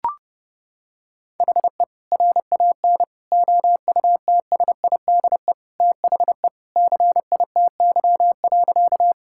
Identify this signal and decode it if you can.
Morse code, heard as a keyed tone, about 30 wpm, 700 Hz. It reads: HE RAN OUTSIDE THE CITY.
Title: Sentences from Top 1000 Words - Review Part 6 - 30wpm